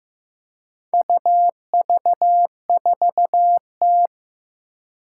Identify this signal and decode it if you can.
Morse code, heard as a keyed tone, about 15 wpm, 700 Hz. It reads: UV4T
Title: Four Challenging Letter-Number Combinations 15wpm